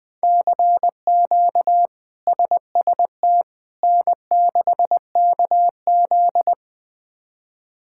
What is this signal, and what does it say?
Morse code, heard as a keyed tone, about 20 wpm, 700 Hz. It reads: CQ SST N6KZ
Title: K1USN SST QSO Practice - 20wpm